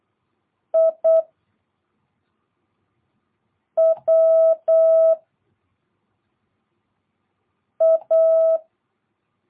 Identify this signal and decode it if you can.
Morse code, heard as a keyed tone, about 8 wpm, 650 Hz. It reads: I W A